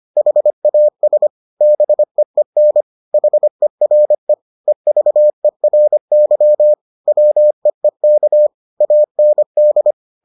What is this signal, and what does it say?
Morse code, heard as a keyed tone, about 25 wpm, 600 Hz. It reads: HAS BEEN HERE EVERY WEEK AND